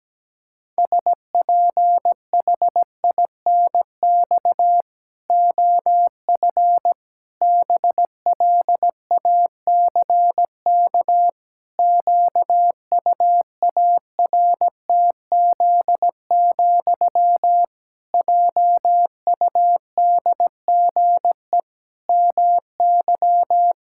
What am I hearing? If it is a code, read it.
Morse code, heard as a keyed tone, about 17 wpm, 700 Hz. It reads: SPHINX OF BLACK QUARTZ, JUDGE MY
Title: Pangram Challenge v2 - 17wpm